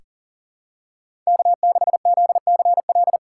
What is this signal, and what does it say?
Morse code, heard as a keyed tone, about 39 wpm, 700 Hz. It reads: K6ZCL
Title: K1USN SST QSO Practice - 40wpm